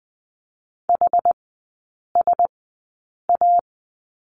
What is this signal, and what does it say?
Morse code, heard as a keyed tone, about 20 wpm, spaced as for 8 wpm, 700 Hz. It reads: HSA
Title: 060 Lesson 11e - Letter h - Three Characters 20wpm